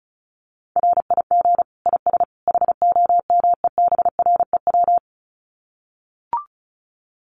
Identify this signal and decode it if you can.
Morse code, heard as a keyed tone, about 34 wpm, 700 Hz. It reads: RIG IS HOMEBREW
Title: Common QSO Phrases 35wpm